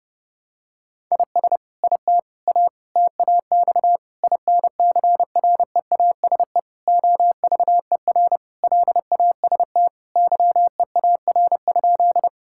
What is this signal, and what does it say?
Morse code, heard as a keyed tone, about 30 wpm, 700 Hz. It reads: IS IT A TAX INCREASE OVER LAST YEAR?
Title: Sentences from Top 1000 Words - Review Part 5 - 30wpm